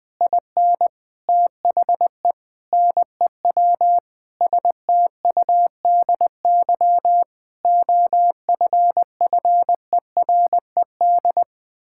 Morse code, heard as a keyed tone, about 20 wpm, 700 Hz. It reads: IN THE NEW STUDY OFFERED
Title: Sentences with 8-Word Phrases - Spoken Beginning - 20wpm